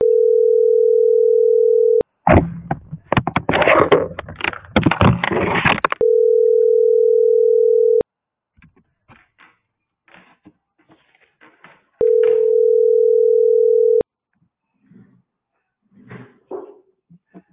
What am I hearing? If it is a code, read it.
Ringback tone (North America), 440 + 480 Hz.